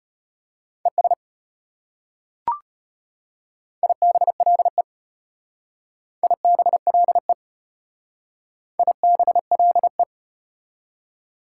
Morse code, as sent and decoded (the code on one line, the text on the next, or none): . ... / .. -... .-.. / .. -... .-.. / .. -... .-.. .
ES IBLE IBLE IBLE